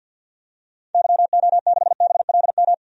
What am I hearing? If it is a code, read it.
Morse code, heard as a keyed tone, about 46 wpm, 700 Hz. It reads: YO6BLM
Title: DXCC-100 Call Signs - 6 Characters - 50wpm